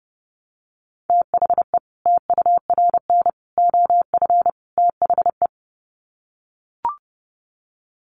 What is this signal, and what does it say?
Morse code, heard as a keyed tone, about 30 wpm, 700 Hz. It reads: THE TURN OF THE
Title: Sets of 4 Words 30wpm